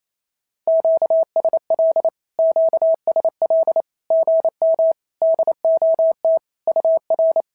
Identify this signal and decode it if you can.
Morse code, heard as a keyed tone, about 27 wpm, 650 Hz. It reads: QSL QSL GM DOT UR
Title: SOTA QSO Practice 28wpm